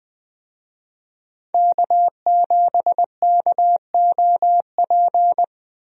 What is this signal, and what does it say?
Morse code, heard as a keyed tone, about 20 wpm, 700 Hz. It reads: K7KOP
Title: USA Call Signs - 5 Characters - Repeated 2x - 20wpm